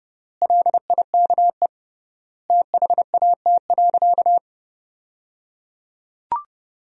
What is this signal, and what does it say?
Morse code, heard as a keyed tone, about 30 wpm, 700 Hz. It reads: LIKE THAT.
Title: 3x Word-Spacing - Sentences from Top 500 Words - 30wpm